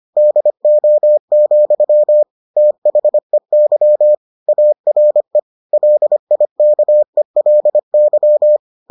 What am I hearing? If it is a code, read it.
Morse code, heard as a keyed tone, about 25 wpm, 600 Hz. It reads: DO, THEY ARE LIKELY